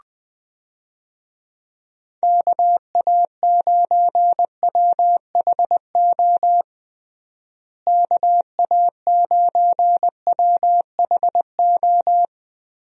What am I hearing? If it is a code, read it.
Morse code, heard as a keyed tone, about 20 wpm, 700 Hz. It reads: KA9WHO KA9WHO
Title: USA Call Signs - 6 Characters - Repeated 2x - 20wpm